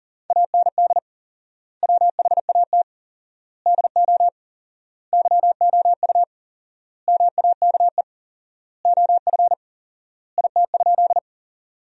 Morse code, heard as a keyed tone, about 40 wpm, 700 Hz. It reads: AND WHAT DO YOU MAKE OF IT?
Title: 4x Word-Spacing - Sentences from Top 100 Words - 40wpm